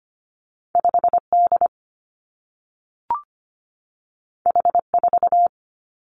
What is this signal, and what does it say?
Morse code, heard as a keyed tone, about 25 wpm, 700 Hz. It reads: 5D H4
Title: Two Challenging Letter-Number Combinations 25wpm